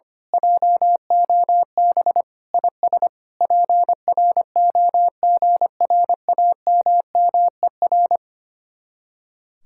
Morse code, heard as a keyed tone, about 24 wpm, 700 Hz. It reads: JOB IS PROGRAMMER
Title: Speed-Racing Common QSO Phrases Encore - 25wpm